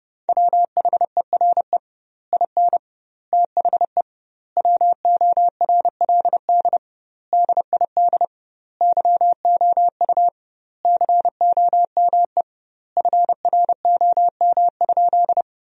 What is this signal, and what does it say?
Morse code, heard as a keyed tone, about 30 wpm, 700 Hz. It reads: WHERE IN THE WORLD DID YOU COME FROM?
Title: 2x Word-Spacing - Sentences from Top 200 Words - 30wpm